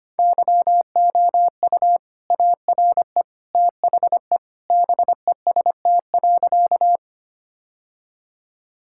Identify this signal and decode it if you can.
Morse code, heard as a keyed tone, about 25 wpm, 700 Hz. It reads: YOU ARE THE BEST.
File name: practice / 25wpm / Sentences with 2-Word Phrases - 25wpm